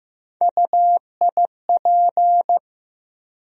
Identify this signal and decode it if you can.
Morse code, heard as a keyed tone, about 15 wpm, 700 Hz. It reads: UIP
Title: USA Call Sign Postfixes - 3 Letters 15wpm